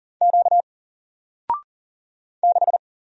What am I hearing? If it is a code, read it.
Morse code, heard as a keyed tone, about 40 wpm, 700 Hz. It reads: Q 6
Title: Single Letter-Number Alt 40wpm